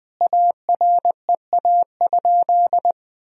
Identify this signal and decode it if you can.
Morse code, heard as a keyed tone, about 21 wpm, 700 Hz. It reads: AREA?
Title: Sentences from Top 1000 Words - Review Part 2 - 20wpm